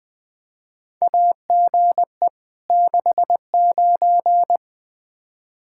Morse code, heard as a keyed tone, about 20 wpm, 700 Hz. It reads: AGE 69
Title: Practice QSO - Exchange 3 - More Info - 20wpm